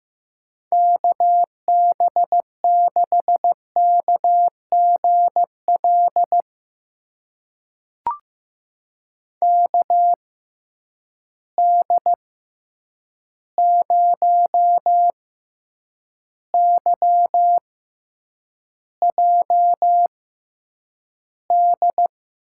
Morse code, heard as a keyed tone, about 15 wpm, spaced as for 8 wpm, 700 Hz. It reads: KB6KGL KD0YJD
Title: USA Call Signs - 6 Characters - 6x Character Spacing - 15wpm